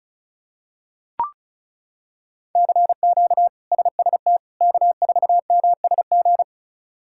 Morse code, heard as a keyed tone, about 36 wpm, 700 Hz. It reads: CQ SST K4MSG
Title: K1USN SST QSO Practice - 35wpm